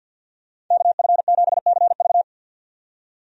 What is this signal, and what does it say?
Morse code, heard as a keyed tone, about 50 wpm, 700 Hz. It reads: KF7CV
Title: USA Call Signs - 5 Characters - 50wpm